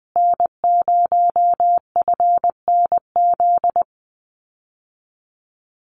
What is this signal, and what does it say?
Morse code, heard as a keyed tone, about 20 wpm, 700 Hz. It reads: N0FNZ
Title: Common QSO Phrases Encore 20wpm